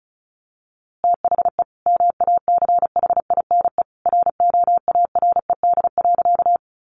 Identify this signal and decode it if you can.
Morse code, heard as a keyed tone, about 34 wpm, 700 Hz. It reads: THE MACHINE ROARED.
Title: A Sound Of Thunder 35wpm